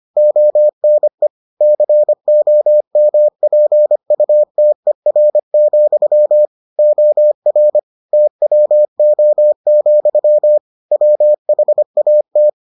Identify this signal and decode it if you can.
Morse code, heard as a keyed tone, about 25 wpm, 600 Hz. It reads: ONE COMPUTER, OR TWO, WHAT